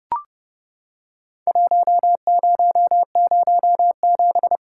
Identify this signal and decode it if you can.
Morse code, heard as a keyed tone, about 30 wpm, 700 Hz. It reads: 1007
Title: Worked All Europe DX Contest - QTC Exchange - 30wpm